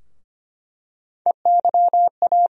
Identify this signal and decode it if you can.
Morse code, heard as a keyed tone, about 25 wpm, 700 Hz. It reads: EYA
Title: USA Call Sign Postfixes - 3 Letters 25wpm